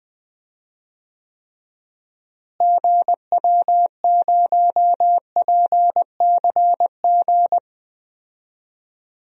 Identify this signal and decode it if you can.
Morse code, heard as a keyed tone, about 20 wpm, 700 Hz. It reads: GW0PCG
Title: DXCC-100 Call Signs - 6 Characters - Repeated 3x - 20wpm